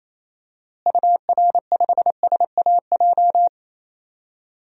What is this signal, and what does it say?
Morse code, heard as a keyed tone, about 28 wpm, 700 Hz. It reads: UR5SAJ
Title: DXCC-100 Call Signs - 6 Characters - Repeated 3x - 28wpm